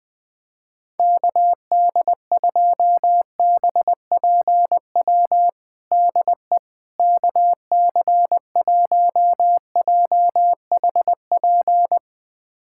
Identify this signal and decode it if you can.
Morse code, heard as a keyed tone, about 20 wpm, 700 Hz. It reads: KD2BPW DE KC1JHP
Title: USA Callsign DE Callsign 20wpm